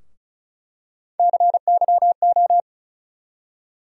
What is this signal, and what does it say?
Morse code, heard as a keyed tone, about 36 wpm, 700 Hz. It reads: CYO
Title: USA Call Sign Postfixes - 3 Letters 35wpm